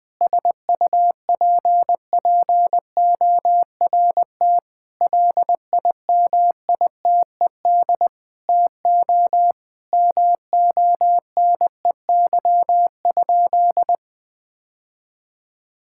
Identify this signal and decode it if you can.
Morse code, heard as a keyed tone, about 20 wpm, 700 Hz. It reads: SUPPORT LIMITED TO MONEY?
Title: Sentences from Top 1000 Words - Review Part 6 - 20wpm